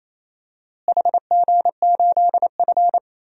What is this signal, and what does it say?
Morse code, heard as a keyed tone, about 28 wpm, 700 Hz. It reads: HG8F
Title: DXCC-100 Call Signs - 4 Characters - Repeated 2x - 28wpm